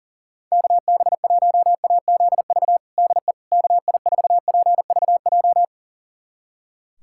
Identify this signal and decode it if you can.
Morse code, heard as a keyed tone, about 40 wpm, 700 Hz. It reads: KB1AZV DE KI4PVJ
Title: USA Callsign DE Callsign - 6 Characters - 40wpm